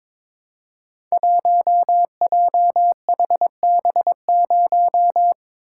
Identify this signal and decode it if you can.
Morse code, heard as a keyed tone, about 22 wpm, 700 Hz. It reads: 1JHB0
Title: Code Groups - 5 Letter:Numbers - 22wpm